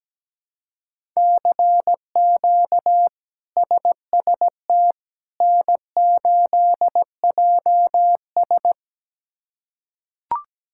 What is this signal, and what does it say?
Morse code, heard as a keyed tone, about 17 wpm, 700 Hz. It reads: CQ SST N8JS
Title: K1USN SST QSO Practice - 17wpm